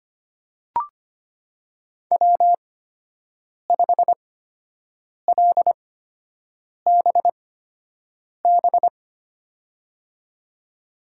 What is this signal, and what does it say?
Morse code, heard as a keyed tone, about 25 wpm, spaced as for 7 wpm, 700 Hz. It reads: W5LBB
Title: USA Call Signs - 5 Characters - 8x Character Spacing - 25wpm